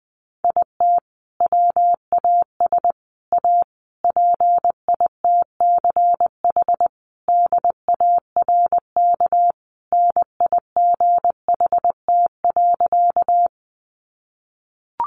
Morse code, heard as a keyed tone, about 20 wpm, 700 Hz. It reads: IT WAS A PITCH DARK NIGHT.
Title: Sentences from Top 1300 Words 20wpm